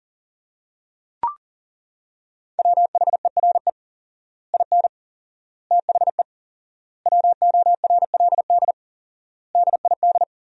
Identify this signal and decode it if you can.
Morse code, heard as a keyed tone, about 40 wpm, 700 Hz. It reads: WHERE IN THE WORLD DID
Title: 4x Word-Spacing - Sentences from Top 200 Words - 40wpm